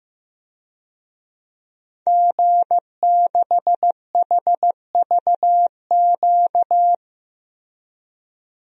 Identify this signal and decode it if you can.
Morse code, heard as a keyed tone, about 15 wpm, 700 Hz. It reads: G6HVQ
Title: DXCC-100 Call Signs - 5 Characters - Repeated 3x - 15wpm